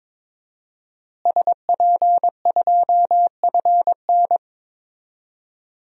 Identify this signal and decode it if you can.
Morse code, heard as a keyed tone, about 22 wpm, 700 Hz. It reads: SP2FN
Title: DXCC-100 Call Signs - 5 Characters - Repeated 2x - 22wpm